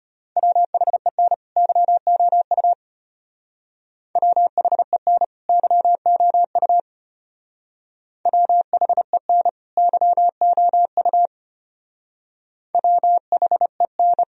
Morse code, heard as a keyed tone, about 32 wpm, 700 Hz. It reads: WHEN YOU WHEN YOU WHEN YOU WHEN